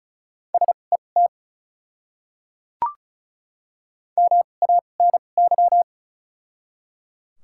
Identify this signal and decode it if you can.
Morse code, heard as a keyed tone, about 34 wpm, spaced as for 25 wpm, 700 Hz. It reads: SET MANY
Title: 2x Character Spacing - Top 200 Words - 35wpm